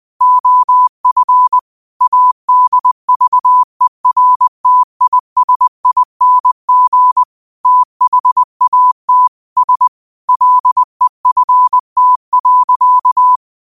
Morse code, heard as a keyed tone, about 20 wpm, 1000 Hz. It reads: OF ADVERTISING THAT S LEFT.